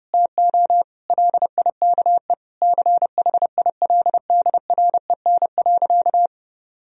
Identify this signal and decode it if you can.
Morse code, heard as a keyed tone, about 32 wpm, 700 Hz. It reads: TO LIKE CHILDREN.
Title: Sentences from Top 1000 Words - Review Part 2 - 30wpm